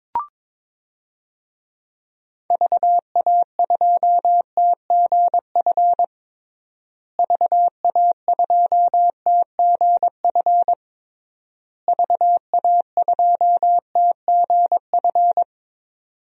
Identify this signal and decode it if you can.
Morse code, heard as a keyed tone, about 22 wpm, 700 Hz. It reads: VA2TGF VA2TGF VA2TGF
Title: DXCC-100 Call Signs - 6 Characters - Repeated 3x - 22wpm